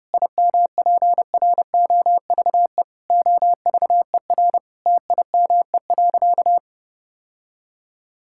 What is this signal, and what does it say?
Morse code, heard as a keyed tone, about 30 wpm, 700 Hz. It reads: IMPROVE OVER TIME.
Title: Sentences from Top 1000 Words - Review Part 5 - 30wpm